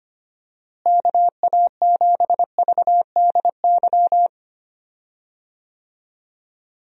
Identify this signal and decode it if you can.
Morse code, heard as a keyed tone, about 25 wpm, 700 Hz. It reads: KA7VDY
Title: USA Call Signs - 6 Characters - Repeated 2x - 25wpm